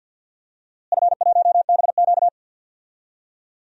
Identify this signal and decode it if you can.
Morse code, heard as a keyed tone, about 47 wpm, 700 Hz. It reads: F1BQ
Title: DXCC-100 Call Signs - 4 Characters - Repeated 2x - 50wpm